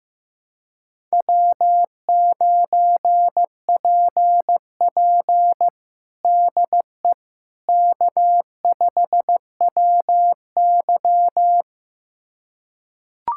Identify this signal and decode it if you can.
Morse code, heard as a keyed tone, about 15 wpm, 700 Hz. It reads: W9PP DE K5WY
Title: USA Callsign DE Callsign - 4 Characters - 15wpm